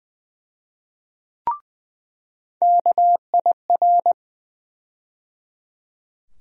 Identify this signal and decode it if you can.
Morse code, heard as a keyed tone, about 20 wpm, 700 Hz. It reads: KIR
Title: USA Call Sign Postfixes - 3 Letters 20wpm